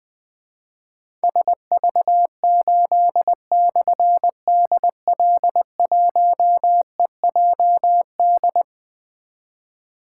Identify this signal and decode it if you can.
Morse code, heard as a keyed tone, about 20 wpm, 700 Hz. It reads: SV8/DL1EJD
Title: Difficult Call Signs 20wpm